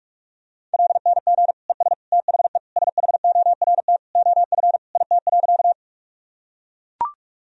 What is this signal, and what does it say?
Morse code, heard as a keyed tone, about 42 wpm, 700 Hz. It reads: LNG ES THE SHORT OF IT.
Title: Sentences from Top 400 Words and Abbreviations 45wpm